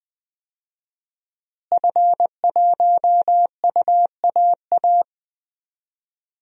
Morse code, heard as a keyed tone, about 21 wpm, 700 Hz. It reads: F1UAA
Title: DXCC-100 Call Signs - 5 Characters - Repeated 3x - 20wpm